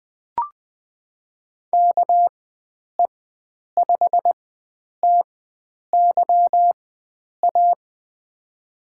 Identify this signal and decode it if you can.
Morse code, heard as a keyed tone, about 20 wpm, spaced as for 9 wpm, 700 Hz. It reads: KE5TYA
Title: USA Call Signs - 4x Character Spacing - 20wpm